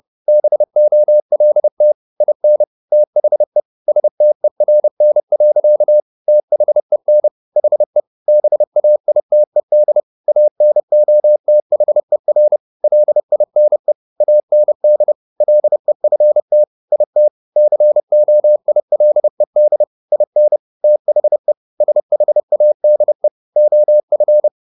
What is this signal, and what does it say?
Morse code, heard as a keyed tone, about 30 wpm, 600 Hz. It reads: BOLT IN THE STERN. THEN HE BAITED ANOTHER LINE AND LEFT IT COILED IN THE SHADE OF
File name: the old man and the sea / the old man and the sea 1 30wpm